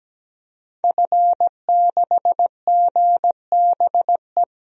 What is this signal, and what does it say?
Morse code, heard as a keyed tone, about 17 wpm, 700 Hz. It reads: F6GBE
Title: DXCC-100 Call Signs - 5 Characters - 17wpm